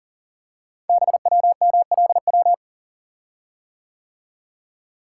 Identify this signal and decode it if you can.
Morse code, heard as a keyed tone, about 39 wpm, 690 Hz. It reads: BWMLW
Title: Code Groups - 5 Letters - 40wpm